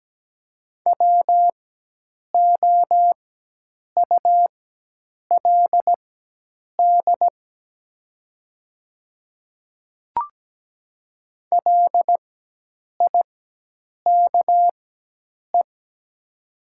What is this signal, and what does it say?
Morse code, heard as a keyed tone, about 17 wpm, spaced as for 8 wpm, 700 Hz. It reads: WOULD LIKE
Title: 4x Character Spacing - Top 100 Words - 17wpm